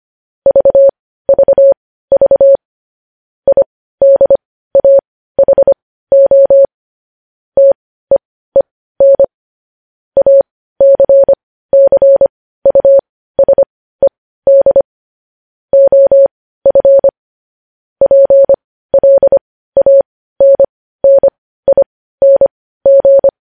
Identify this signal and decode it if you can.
Morse code, heard as a keyed tone, about 25 wpm, spaced as for 15 wpm, 550 Hz. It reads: VVV IDAHO TEEN ACCUSED OF PLANNING
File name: podcast / News headlines-1x-25x15